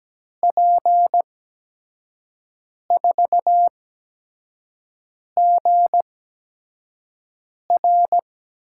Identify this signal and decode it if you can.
Morse code, heard as a keyed tone, about 17 wpm, spaced as for 5 wpm, 700 Hz. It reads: P4GR
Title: USA Call Signs - 5 Characters - 8x Character Spacing - 17wpm